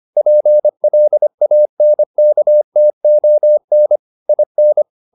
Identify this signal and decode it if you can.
Morse code, heard as a keyed tone, about 25 wpm, 600 Hz. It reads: PLANKTON IN
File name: the old man and the sea part 1 25wpm